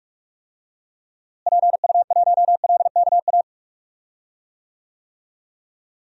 Morse code, heard as a keyed tone, about 46 wpm, 700 Hz. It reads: PU1LKA
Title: DXCC-100 Call Signs - 6 Characters - Repeated 3x - 45wpm